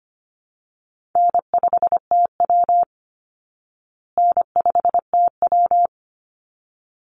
Morse code, heard as a keyed tone, about 25 wpm, 700 Hz. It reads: N5TW N5TW
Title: Easy Call Signs Repeated 2x - 25wpm